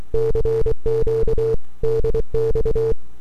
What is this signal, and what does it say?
Morse code, heard as a keyed tone, about 24 wpm, 450 Hz. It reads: CQ DX
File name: n6tr / 12Oct2009-160